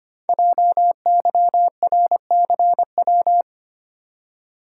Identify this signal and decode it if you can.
Morse code, heard as a keyed tone, about 25 wpm, 700 Hz. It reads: JYRCW
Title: Code Groups - 5 Letter:Numbers - 25wpm